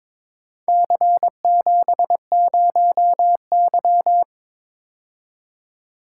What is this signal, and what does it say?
Morse code, heard as a keyed tone, about 23 wpm, 700 Hz. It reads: C70Y
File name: practice / 22wpm / Four Letter-Number Combinations - 22wpm